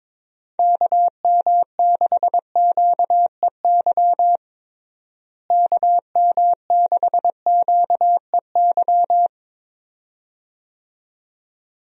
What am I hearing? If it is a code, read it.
Morse code, heard as a keyed tone, about 22 wpm, 700 Hz. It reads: KM6QEY KM6QEY
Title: USA Call Signs - 6 Characters - Repeated 2x - 22wpm